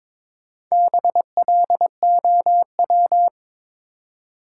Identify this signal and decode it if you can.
Morse code, heard as a keyed tone, about 22 wpm, 700 Hz. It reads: BLOW
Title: Four Letter Words - 22wpm